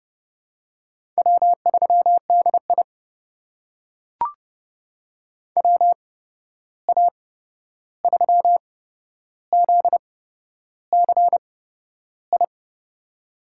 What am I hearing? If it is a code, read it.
Morse code, heard as a keyed tone, about 31 wpm, spaced as for 11 wpm, 700 Hz. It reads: W3DI WA3ZCI
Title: USA Call Signs - 8x Character Spacing - 30wpm